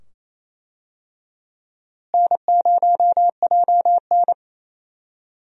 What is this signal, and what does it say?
Morse code, heard as a keyed tone, about 28 wpm, 700 Hz. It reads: N0JN
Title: USA Call Signs - 4 Characters - Repeated 2x - 28wpm